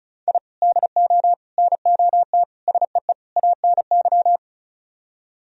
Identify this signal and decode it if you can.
Morse code, heard as a keyed tone, about 43 wpm, 700 Hz. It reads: I DO NOT SEE ANY
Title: Speed-Racing - Sets of 5 Words 35wpm